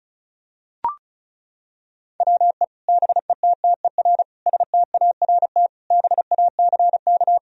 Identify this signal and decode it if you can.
Morse code, heard as a keyed tone, about 36 wpm, 700 Hz. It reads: WE BETTER START BACK
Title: Sentences from Top 1000 Words - Review Part 1 - 35wpm